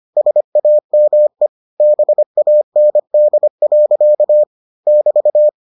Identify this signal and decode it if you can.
Morse code, heard as a keyed tone, about 25 wpm, 600 Hz. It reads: SAME BAND. <BT>